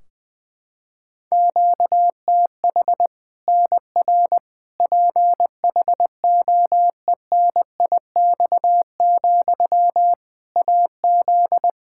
Morse code, heard as a keyed tone, about 20 wpm, 700 Hz. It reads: QTH NR PHOENIX, AZ
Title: Practice QSO - Exchange 1 - Beginning - 20wpm